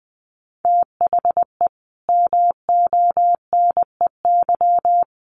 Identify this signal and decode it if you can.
Morse code, heard as a keyed tone, about 19 wpm, 700 Hz. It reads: THE MONEY
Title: Sentences from Top 1500 Words 20wpm